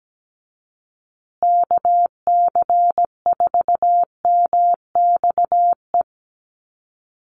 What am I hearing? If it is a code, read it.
Morse code, heard as a keyed tone, about 17 wpm, 700 Hz. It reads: KC4MXE
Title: USA Call Signs Repeated 3x 17wpm